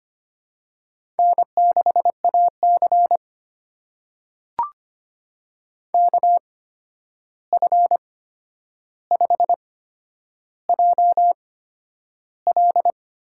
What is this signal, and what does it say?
Morse code, heard as a keyed tone, about 25 wpm, spaced as for 10 wpm, 700 Hz. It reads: N6AC KF5JL
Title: USA Call Signs - 8x Character Spacing - 25wpm